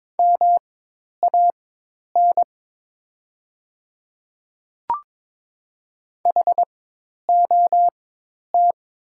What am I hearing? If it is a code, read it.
Morse code, heard as a keyed tone, about 22 wpm, spaced as for 10 wpm, 700 Hz. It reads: MAN HOT
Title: 4x Character Spacing - Top 200 Words - 22wpm